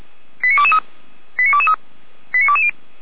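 Five-tone selective call (ZVEI), heard as quick runs of five tones, tones of about 80 ms. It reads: 89223 89223 89220
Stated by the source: ZVEI five-tone call 89022 received on an SDR.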